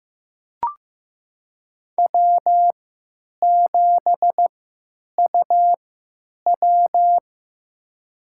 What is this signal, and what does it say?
Morse code, heard as a keyed tone, about 15 wpm, spaced as for 9 wpm, 700 Hz. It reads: W7UW
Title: USA Call Signs - 4 Characters - 3x Character Spacing - 15wpm